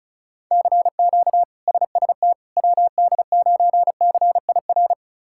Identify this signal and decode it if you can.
Morse code, heard as a keyed tone, about 36 wpm, 700 Hz. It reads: CQ SST WD9CIR
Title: K1USN SST QSO Practice - 35wpm